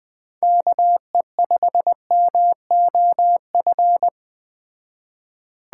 Morse code, heard as a keyed tone, about 20 wpm, 700 Hz. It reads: KE5MOF
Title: Speed Racing - USA Call Signs 17wpm